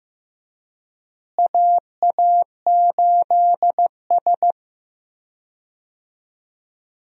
Morse code, heard as a keyed tone, about 16 wpm, 700 Hz. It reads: AA8S
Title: USA Call Signs - 4 Characters - Repeated 3x - 15wpm